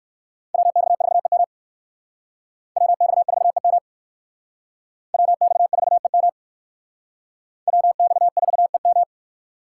Morse code, heard as a keyed tone, about 56 wpm, 700 Hz. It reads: WX4EM WX4EM WX4EM WX4EM
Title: Speed Racing - USA 2x2 Call Signs 45wpm